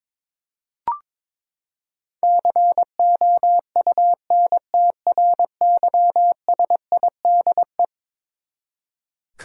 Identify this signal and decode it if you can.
Morse code, heard as a keyed tone, about 21 wpm, 700 Hz. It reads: COUNTRYSIDE
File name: practice / 22wpm / Eleven Letter Words - 22wpm